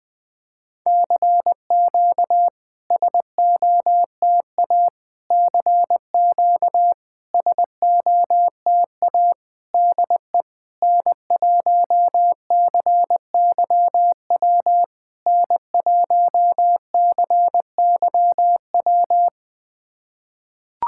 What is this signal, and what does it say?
Morse code, heard as a keyed tone, about 21 wpm, 700 Hz. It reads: CQ SOTA CQ SOTA DE N1CYW N1CYW
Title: SOTA QSO Practice 20wpm